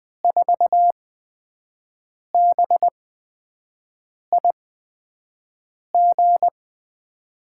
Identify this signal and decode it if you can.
Morse code, heard as a keyed tone, about 20 wpm, spaced as for 5 wpm, 700 Hz. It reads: 4BIG
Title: USA Call Signs - 5 Characters - 8x Character Spacing - 20wpm